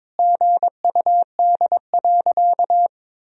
Morse code, heard as a keyed tone, about 21 wpm, 700 Hz. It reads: GUD.
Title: Sentences from Top 200 Words and Abbreviations 22wpm